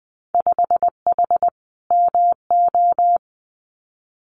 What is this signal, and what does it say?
Morse code, heard as a keyed tone, about 20 wpm, 700 Hz. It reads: 5H MO
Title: Winter Field Day Exchanges Encore 20wpm